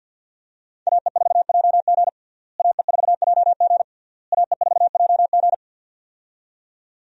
Morse code, heard as a keyed tone, about 47 wpm, 700 Hz. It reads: AE4JG AE4JG AE4JG
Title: USA Call Signs - 5 Characters - Repeated 3x - 50wpm